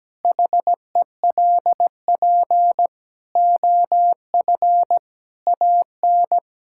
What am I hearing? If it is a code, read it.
Morse code, heard as a keyed tone, about 17 wpm, 700 Hz. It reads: HELP OF AN
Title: Sentences from Top 1000 Words - Review Part 5 - 17wpm